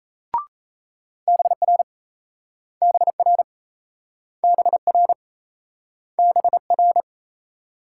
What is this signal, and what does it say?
Morse code, heard as a keyed tone, about 38 wpm, 700 Hz. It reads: BR BR BR BR